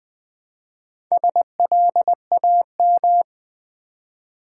Morse code, heard as a keyed tone, about 20 wpm, 700 Hz. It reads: SLAM
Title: Build A Word - 2 Letters At A Time - 4-Letter Words - 20wpm